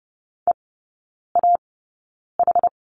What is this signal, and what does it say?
Morse code, heard as a keyed tone, about 30 wpm, spaced as for 9 wpm, 700 Hz. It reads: EAH
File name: Lesson 16e - Letter c - Three Characters 30wpm